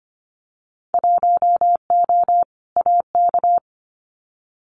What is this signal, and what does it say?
Morse code, heard as a keyed tone, about 25 wpm, 700 Hz. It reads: 1O AK
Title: Winter Field Day Exchanges Encore 25wpm